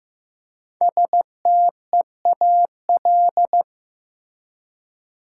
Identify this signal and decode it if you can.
Morse code, heard as a keyed tone, about 15 wpm, 700 Hz. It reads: STEAL
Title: Five Letter Words - 15wpm